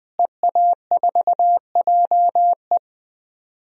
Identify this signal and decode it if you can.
Morse code, heard as a keyed tone, about 20 wpm, 700 Hz. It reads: EA4JE
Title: DXCC-100 Call Signs - 5 Characters - Repeated 2x - 20wpm